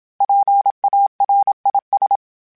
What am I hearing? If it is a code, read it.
Morse code, heard as a keyed tone, about 27 wpm, 800 Hz. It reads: PARIS
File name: PARIS Calibration / PARIS mp3s / PARIS@30